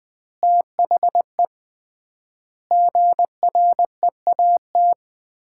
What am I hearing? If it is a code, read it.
Morse code, heard as a keyed tone, about 20 wpm, 700 Hz. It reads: THE GREAT
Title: 3x Word-Spacing - Sets of 4 Words - 20wpm